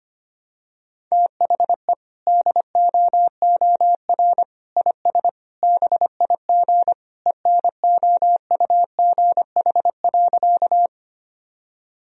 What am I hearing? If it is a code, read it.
Morse code, heard as a keyed tone, about 25 wpm, 700 Hz. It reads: THE DOOR IS BIG ENOUGH.
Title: Sentences from Top 1000 Words - Review Part 1 - 25wpm